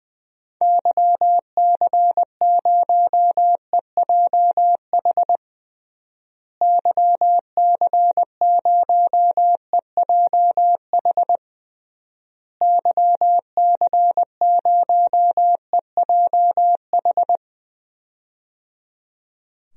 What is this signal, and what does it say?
Morse code, heard as a keyed tone, about 20 wpm, 700 Hz. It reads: YC0EJH YC0EJH YC0EJH
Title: DXCC-100 Call Signs - 6 Characters - Repeated 3x - 20wpm